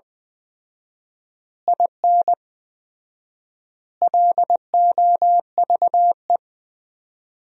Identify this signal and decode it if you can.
Morse code, heard as a keyed tone, about 19 wpm, 700 Hz. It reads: IN LOVE
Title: 4x Word-Spacing - Sentences from Top 400 Words - 20wpm